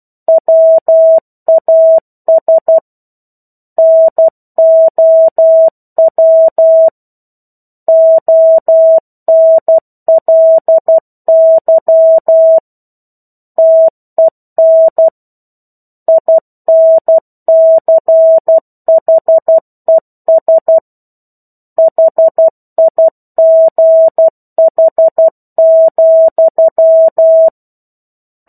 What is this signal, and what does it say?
Morse code, heard as a keyed tone, about 12 wpm, 650 Hz. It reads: WAS NOW ONLY TEN INCHES HIGH,
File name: 01-Chapter-01 12 WPM